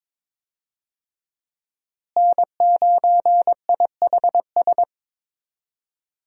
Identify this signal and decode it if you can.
Morse code, heard as a keyed tone, about 21 wpm, 700 Hz. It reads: N9IHS
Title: USA Call Signs - 5 Characters - Repeated 3x - 22wpm